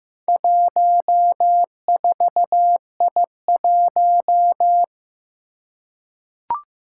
Morse code, heard as a keyed tone, about 15 wpm, 700 Hz. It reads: 14I1
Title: Four Challenging Letter-Number Combinations 15wpm